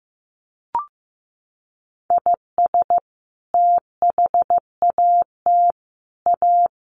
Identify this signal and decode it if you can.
Morse code, heard as a keyed tone, about 15 wpm, 700 Hz. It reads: IS THAT A
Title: Sentences from Top 1200 Words 15wpm